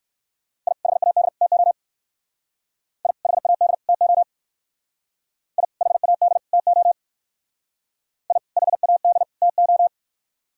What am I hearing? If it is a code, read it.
Morse code, heard as a keyed tone, about 57 wpm, 700 Hz. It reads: I HAD TO I HAD TO I HAD TO I HAD TO